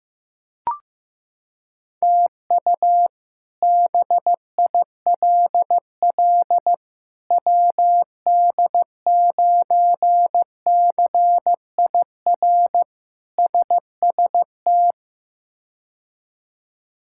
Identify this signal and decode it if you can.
Morse code, heard as a keyed tone, about 16 wpm, 700 Hz. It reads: TU BILL WD9CIR SST
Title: K1USN SST QSO Practice - 15wpm